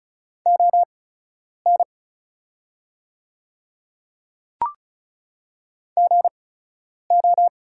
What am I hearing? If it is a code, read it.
Morse code, heard as a keyed tone, about 36 wpm, spaced as for 10 wpm, 700 Hz. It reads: ON GO
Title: 8x Character Spacing - Top 100 Words - 35wpm